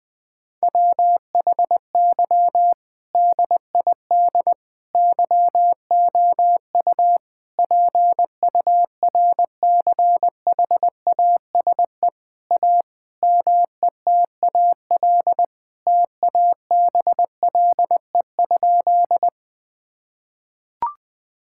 Morse code, heard as a keyed tone, about 20 wpm, 700 Hz. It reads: WHY DID YOU PURCHASE A METAL TABLE?
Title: Sentences from Top 1000 Words - Review Part 6 - 20wpm